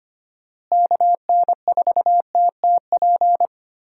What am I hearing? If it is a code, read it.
Morse code, heard as a keyed tone, about 25 wpm, 700 Hz. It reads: KN4TTP
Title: USA Call Signs - 6 Characters - Repeated 2x - 25wpm